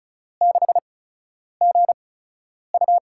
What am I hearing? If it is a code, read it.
Morse code, heard as a keyed tone, about 36 wpm, spaced as for 10 wpm, 700 Hz. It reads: 6GU
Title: USA Call Signs - 5 Characters - 8x Character Spacing - 35wpm